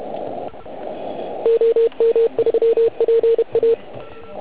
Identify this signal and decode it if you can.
Morse code, heard as a keyed tone, about 32 wpm, 460 Hz. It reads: OM3PA
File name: Archiv / CUC129 / CUC129 pileup